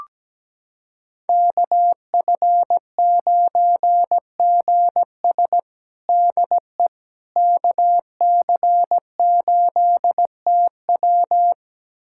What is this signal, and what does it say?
Morse code, heard as a keyed tone, about 17 wpm, 700 Hz. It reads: KF9GS DE KC8TW
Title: USA Callsign DE Callsign - 5 Characters - 17wpm